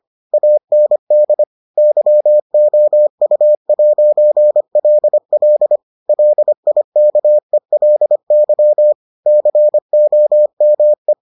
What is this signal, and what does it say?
Morse code, heard as a keyed tone, about 25 wpm, 600 Hz. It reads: AND YOU'LL LIKELY COME